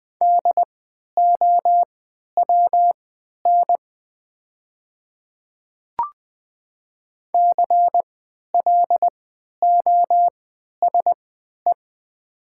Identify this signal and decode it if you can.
Morse code, heard as a keyed tone, about 20 wpm, spaced as for 11 wpm, 700 Hz. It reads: DOWN CLOSE
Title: 3x Character Spacing - Top 500 Words - 20wpm